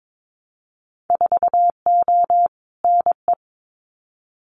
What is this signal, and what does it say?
Morse code, heard as a keyed tone, about 22 wpm, 700 Hz. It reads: 4O NE
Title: Winter Field Day Exchanges Encore 22wpm